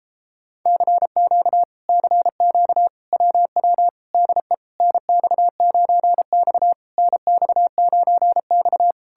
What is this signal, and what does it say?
Morse code, heard as a keyed tone, about 38 wpm, 700 Hz. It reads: CQ CQ WW DE NX9X NX9X